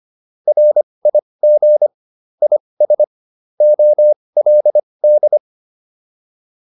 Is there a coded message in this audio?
Yes: Morse code from 0.5 to 5.4 s, heard as a keyed tone, about 25 wpm, spaced as for 20 wpm, 600 Hz. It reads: RIG IS OLD